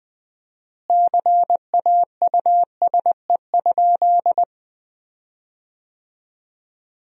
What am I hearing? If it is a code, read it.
Morse code, heard as a keyed tone, about 21 wpm, 700 Hz. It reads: CAUSE?
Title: 4x Word-Spacing - Sentences from Top 200 Words - 20wpm